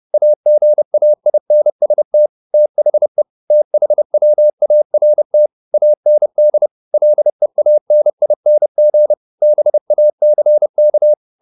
Morse code, heard as a keyed tone, about 30 wpm, 600 Hz. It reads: AGAINST THE THWART AND LEANING BACK
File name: the old man and the sea / the old man and the sea 1 30wpm